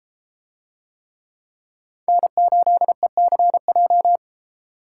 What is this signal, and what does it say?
Morse code, heard as a keyed tone, about 32 wpm, 700 Hz. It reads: N8ECJ